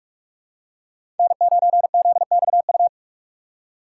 Morse code, heard as a keyed tone, about 44 wpm, 700 Hz. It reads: N9ZXU
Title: USA Call Signs - 5 Characters - 6x Character Spacing - 45wpm